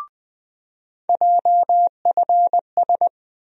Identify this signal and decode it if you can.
Morse code, heard as a keyed tone, about 20 wpm, 700 Hz. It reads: JFS
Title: USA Call Sign Postfixes - 3 Letters 20wpm